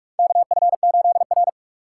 Morse code, heard as a keyed tone, about 44 wpm, 700 Hz. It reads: KF8R
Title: USA Call Signs - 4 Characters - 6x Character Spacing - 45wpm